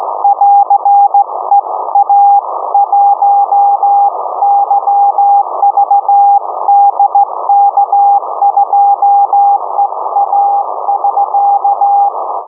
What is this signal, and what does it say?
Morse code, heard as a keyed tone, about 16 wpm, 810 Hz. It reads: <AR> EA1YVD K2EA <SK>
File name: uploader / 50086 khz cw